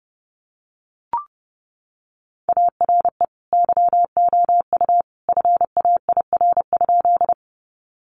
Morse code, heard as a keyed tone, about 28 wpm, 700 Hz. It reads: ARE YOU FAIR?
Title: Sentences from Top 1300 Words 30wpm